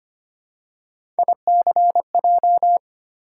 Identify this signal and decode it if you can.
Morse code, heard as a keyed tone, about 25 wpm, 700 Hz. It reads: ICJ
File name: Three Letter-Number Combinations - 25wpm